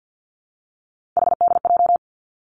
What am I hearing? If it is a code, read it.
Morse code, heard as a keyed tone, about 50 wpm, 700 Hz. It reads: HDJ
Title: Three Challenging Letter-Number Combinations 50wpm